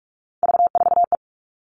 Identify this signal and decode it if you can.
Morse code, heard as a keyed tone, about 46 wpm, 700 Hz. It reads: V4E
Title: Three Challenging Letter-Number Combinations 45wpm